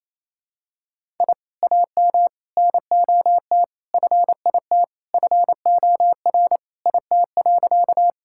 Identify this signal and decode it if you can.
Morse code, heard as a keyed tone, about 28 wpm, 700 Hz. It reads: I AM NOT FIT FOR IT.
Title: Sentences from Top 1000 Words - Review Part 5 - 28wpm